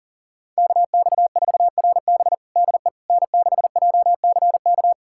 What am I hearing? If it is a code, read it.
Morse code, heard as a keyed tone, about 40 wpm, 700 Hz. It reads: KX4RB DE N6JCK
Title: USA Callsign DE Callsign - 5 Characters - 40wpm